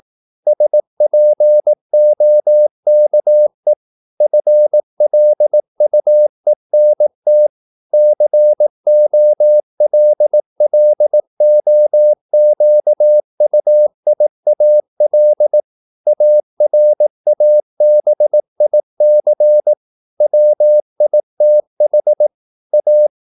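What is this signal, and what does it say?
Morse code, heard as a keyed tone, about 18 wpm, 600 Hz. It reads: SPOKE FLUENT COLLOQUIAL ARABIC WITH A
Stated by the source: Morse practice at 18 wpm, English text, from CW med Gnister.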